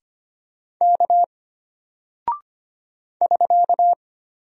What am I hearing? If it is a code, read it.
Morse code, heard as a keyed tone, about 25 wpm, 700 Hz. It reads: K <SK>
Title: Single Character (Letter:Number:Punctuation:Prosign) Alt 25wpm